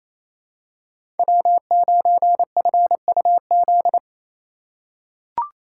Morse code, heard as a keyed tone, about 28 wpm, 700 Hz. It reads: W9FUZ
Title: USA Call Signs - 2x Character Spacing - 28wpm